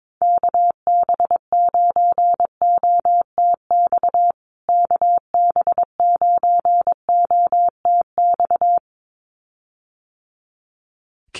Morse code, heard as a keyed tone, about 22 wpm, 700 Hz. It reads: KB9OTX KB9OTX
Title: USA Call Signs Repeated 2x 22wpm